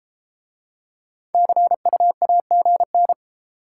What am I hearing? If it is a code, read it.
Morse code, heard as a keyed tone, about 29 wpm, 700 Hz. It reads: CU AGN